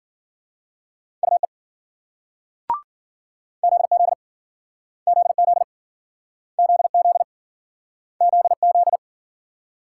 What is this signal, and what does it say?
Morse code, heard as a keyed tone, about 53 wpm, 700 Hz. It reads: UE ZZ ZZ ZZ ZZ